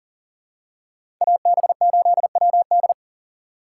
Morse code, heard as a keyed tone, about 40 wpm, 700 Hz. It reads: AB8WD
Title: USA Call Signs - 5 Characters - 4x Character Spacing - 40wpm